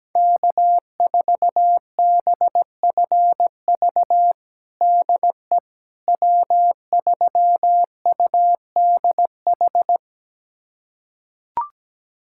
Morse code, heard as a keyed tone, about 17 wpm, 700 Hz. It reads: K4BFV DE W3UDH
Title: USA Callsign DE Callsign - 5 Characters - 17wpm